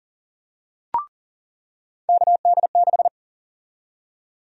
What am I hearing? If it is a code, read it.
Morse code, heard as a keyed tone, about 37 wpm, 700 Hz. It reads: KD6
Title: USA Call Sign Prefixes - 40wpm